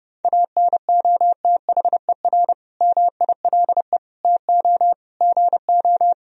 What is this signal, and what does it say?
Morse code, heard as a keyed tone, about 30 wpm, 700 Hz. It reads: ANOTHER MILE TO GO
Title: Sentences from Top 1000 Words - Review Part 1 - 30wpm